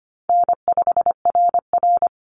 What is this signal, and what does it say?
Morse code, heard as a keyed tone, about 25 wpm, 700 Hz. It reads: N5RR
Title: Easy Call Signs - 25wpm